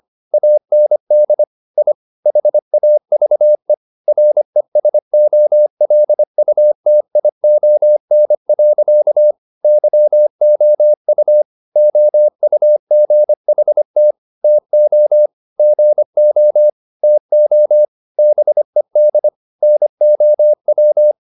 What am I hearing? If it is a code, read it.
Morse code, heard as a keyed tone, about 25 wpm, 600 Hz. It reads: AND I HAVE RESOLUTION. YOU OUGHT TO GO TO BED NOW
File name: the old man and the sea part 1 25wpm